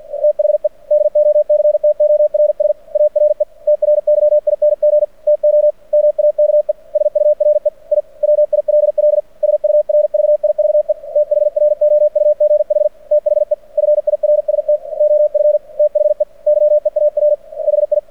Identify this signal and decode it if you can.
Morse code, heard as a keyed tone, about 48 wpm, 600 Hz. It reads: THE DOCTORS ARE TRYING TO MAKE SURE I WILL SURVIVE THROUGH THE FIRST OF THE YEAR HI
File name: archived / CRSnet